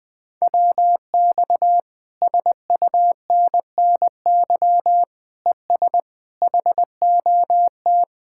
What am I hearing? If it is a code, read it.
Morse code, heard as a keyed tone, about 20 wpm, 700 Hz. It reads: WX SUNNY ES HOT
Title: Speed-Racing Common QSO Phrases Encore - 20wpm